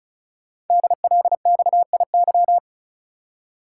Morse code, heard as a keyed tone, about 36 wpm, 690 Hz. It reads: DLXIY
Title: Code Groups - 5 Letters - 35wpm